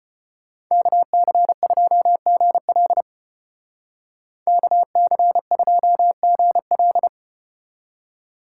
Morse code, heard as a keyed tone, about 34 wpm, 700 Hz. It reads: KC2GL KC2GL